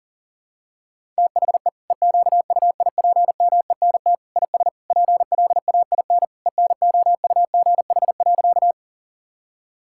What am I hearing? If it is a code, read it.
Morse code, heard as a keyed tone, about 40 wpm, 700 Hz. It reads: THE EQUIPMENT IS PLAIN ENOUGH.
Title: Sentences from Top 1000 Words - Review Part 5 - 40wpm